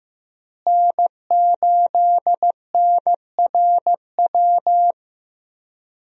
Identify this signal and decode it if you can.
Morse code, heard as a keyed tone, about 15 wpm, 700 Hz. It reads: N8NRW
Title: USA Call Signs - 5 Characters - Repeated 2x - 15wpm